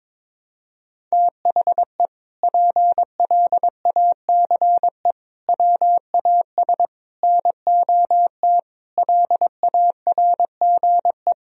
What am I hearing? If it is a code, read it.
Morse code, heard as a keyed tone, about 22 wpm, 700 Hz. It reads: THE PLACE WAS NOT LARGE
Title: Sentences from Top 1000 Words - Review Part 1 - 22wpm